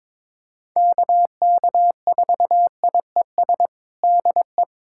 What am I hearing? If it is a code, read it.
Morse code, heard as a keyed tone, about 22 wpm, 700 Hz. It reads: KK4IES DE
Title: USA Callsign DE Callsign 22wpm